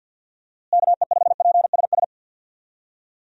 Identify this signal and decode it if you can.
Morse code, heard as a keyed tone, about 51 wpm, 700 Hz. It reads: KE5PSS
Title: USA Call Signs - 6 Characters - Repeated 3x - 50wpm